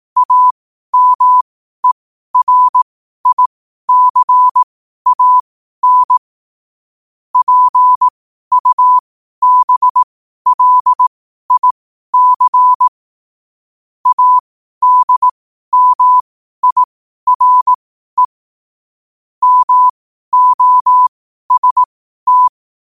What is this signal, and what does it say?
Morse code, heard as a keyed tone, about 18 wpm, spaced as for 12 wpm, 1000 Hz. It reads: AMERICAN PUBLIC ADMIRE MOST
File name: qotdfiles / 12wpm